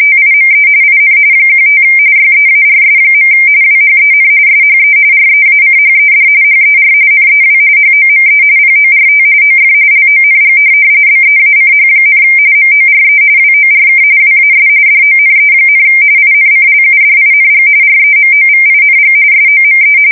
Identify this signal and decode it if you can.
RTTY, about 45.45 baud. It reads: TIME IN 15 YEARS. WHEN SHE REACHED HER NEW HOME PORT OF LONG BEACH, MORE WORK WAS DONE, INCLUDING REMOVING SOME
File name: fsk170 / USS-New-Jersey